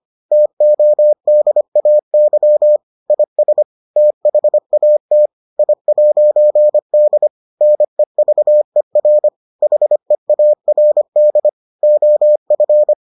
Morse code, heard as a keyed tone, about 25 wpm, 600 Hz. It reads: TODAY IS THAT I'D NEVER HEARD OF